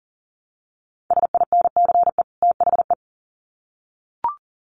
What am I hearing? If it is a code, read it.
Morse code, heard as a keyed tone, about 40 wpm, 700 Hz. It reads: SINCE THE
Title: Sets of 2 Words 40wpm